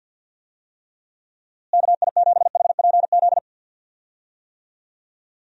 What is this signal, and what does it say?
Morse code, heard as a keyed tone, about 48 wpm, 700 Hz. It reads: KI7HPZ
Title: USA Call Signs - 6 Characters - Repeated 2x - 50wpm